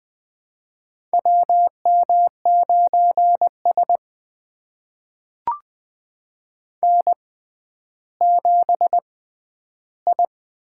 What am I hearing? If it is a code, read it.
Morse code, heard as a keyed tone, about 20 wpm, spaced as for 11 wpm, 700 Hz. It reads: WM9S N7I
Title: USA Call Signs - 4 Characters - 6x Character Spacing - 20wpm